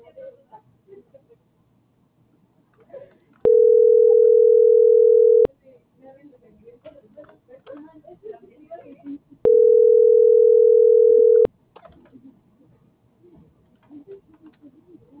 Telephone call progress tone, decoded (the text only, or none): ringback tone